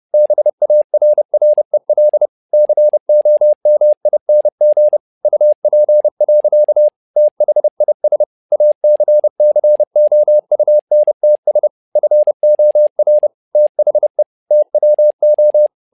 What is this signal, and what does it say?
Morse code, heard as a keyed tone, about 30 wpm, 600 Hz. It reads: BARREL COMING UP. THIS ACCOUNTS FOR THE TWO
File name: learning center / tower trouble / tower trouble 30wpm